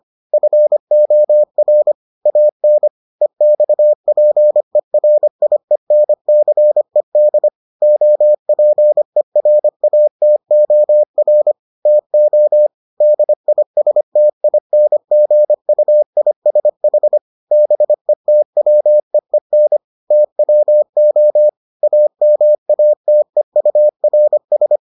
Morse code, heard as a keyed tone, about 25 wpm, 600 Hz. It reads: FOR AN EXPERIENCED OPERATOR TO DISTINGUISH BETWEEN TWO AMATEURS